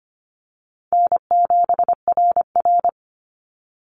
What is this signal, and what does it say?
Morse code, heard as a keyed tone, about 25 wpm, 690 Hz. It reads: N7RR
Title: NA SOTA Activator Callsigns 25wpm